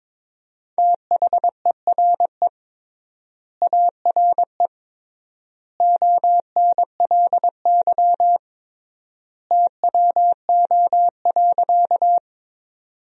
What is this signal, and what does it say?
Morse code, heard as a keyed tone, about 22 wpm, 700 Hz. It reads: THERE ARE ONLY TWO.
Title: 3x Word-Spacing - Sentences from Top 200 Words - 22wpm